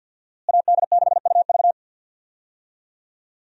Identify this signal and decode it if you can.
Morse code, heard as a keyed tone, about 47 wpm, 700 Hz. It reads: AD6UV
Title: USA Call Signs - 5 Characters - Repeated 3x - 50wpm